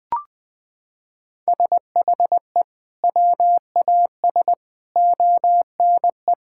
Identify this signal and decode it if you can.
Morse code, heard as a keyed tone, about 20 wpm, 700 Hz. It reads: SHE WAS ONE
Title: Sentences with 4-Word Phrases - 20wpm